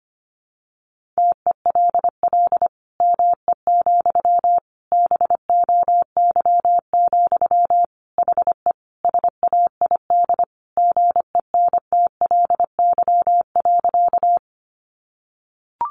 Morse code, heard as a keyed tone, about 25 wpm, 700 Hz. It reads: TELL ME, BOY, HE SAID GENTLY.
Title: Sentences from Top 1500 Words 25wpm